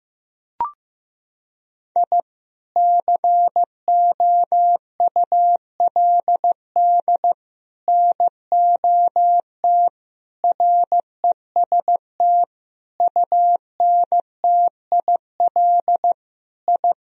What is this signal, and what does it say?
Morse code, heard as a keyed tone, about 15 wpm, 700 Hz. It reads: I COULD NOT REST UNTIL I
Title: Sentences from Top 1000 Words - Review Part 1 - 15wpm